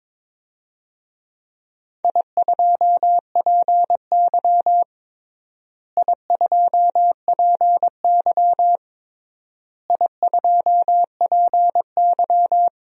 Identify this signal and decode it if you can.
Morse code, heard as a keyed tone, about 22 wpm, 700 Hz. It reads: I2PY I2PY I2PY
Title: DXCC-100 Call Signs - 4 Characters - Repeated 3x - 22wpm